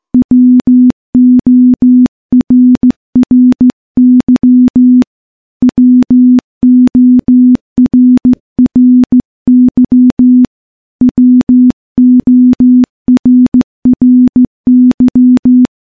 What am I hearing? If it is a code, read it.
Morse code, heard as a keyed tone, about 16 wpm, 250 Hz. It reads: WORRY WORRY WORRY
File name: morse-training-15WPM